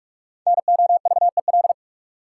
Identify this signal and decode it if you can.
Morse code, heard as a keyed tone, about 44 wpm, 700 Hz. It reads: NOVEL
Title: Sentences from Top 1300 Words - New Word Drill Alt - 45wpm